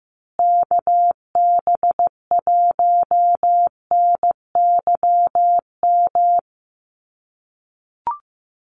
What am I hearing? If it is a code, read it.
Morse code, heard as a keyed tone, about 15 wpm, 700 Hz. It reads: KB1NYM
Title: USA Call Signs Repeated 2x 15wpm